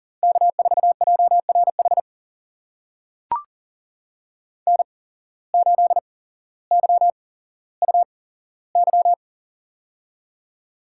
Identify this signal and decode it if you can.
Morse code, heard as a keyed tone, about 40 wpm, spaced as for 17 wpm, 700 Hz. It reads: K4JRH N8YUY
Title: USA Call Signs - 5 Characters - 8x Character Spacing - 40wpm